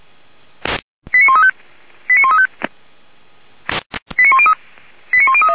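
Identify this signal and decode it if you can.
Five-tone selective call (ZVEI), heard as quick runs of five tones, tones of about 70 ms. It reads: 89135 89135 89103
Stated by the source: Five-tone ZVEI selective call received on an SDR.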